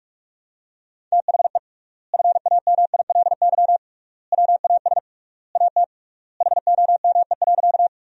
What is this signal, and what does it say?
Morse code, heard as a keyed tone, about 46 wpm, 700 Hz. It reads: THE FAMILY WAS AT HOME.
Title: 3x Word-Spacing - Sentences from Top 400 Words - 45wpm